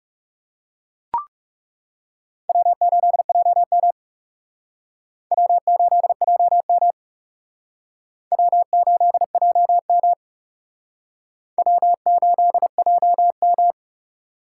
Morse code, heard as a keyed tone, about 36 wpm, 700 Hz. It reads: W8JM W8JM W8JM W8JM
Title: Speed Racing - USA 1x2 Call Signs 30wpm